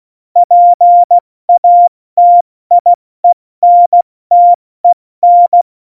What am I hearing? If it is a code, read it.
Morse code, heard as a keyed tone, about 16 wpm, 700 Hz. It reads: PATIENTEN
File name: Morsekurs / Witze / Witz 27. Februar 2018, 70 F80